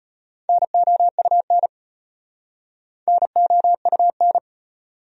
Morse code, heard as a keyed tone, about 33 wpm, 700 Hz. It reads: NOUN NOUN